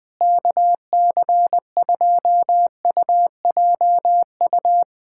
Morse code, heard as a keyed tone, about 20 wpm, 700 Hz. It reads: KC2UJU
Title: USA Callsign DE Callsign - 6 Characters - 20wpm